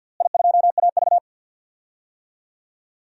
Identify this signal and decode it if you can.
Morse code, heard as a keyed tone, about 53 wpm, 700 Hz. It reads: I2AV
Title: DXCC-100 Call Signs - 4 Characters - Repeated 2x - 50wpm